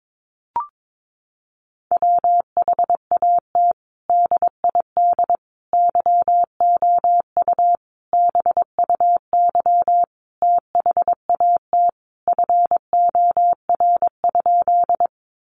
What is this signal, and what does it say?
Morse code, heard as a keyed tone, about 22 wpm, 700 Hz. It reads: WHAT DID YOU BUY THAT FOR?
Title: Sentences from Top 700 Words 22wpm